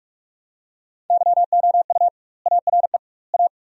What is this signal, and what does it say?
Morse code, heard as a keyed tone, about 46 wpm, 700 Hz. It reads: YOU ARE A
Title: 2x Word-Spacing - Sentences from Top 200 Words - 45wpm